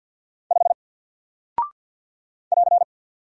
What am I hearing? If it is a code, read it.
Morse code, heard as a keyed tone, about 50 wpm, 700 Hz. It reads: 5 <AR>
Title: Single Character (Letter:Number:Punctuation:Prosign) 50wpm